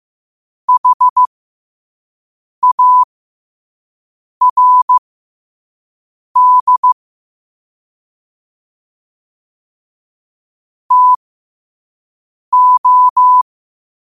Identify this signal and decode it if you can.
Morse code, heard as a keyed tone, about 15 wpm, spaced as for 5 wpm, 1000 Hz. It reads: HARD TO
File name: qotdfiles / 5wpm